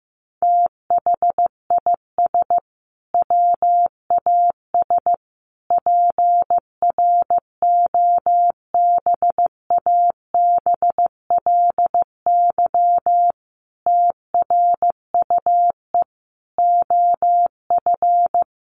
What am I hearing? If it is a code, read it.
Morse code, heard as a keyed tone, about 15 wpm, 700 Hz. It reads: THIS WAS PROBABLY TRUE OF
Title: Sentences from Top 700 Words 15wpm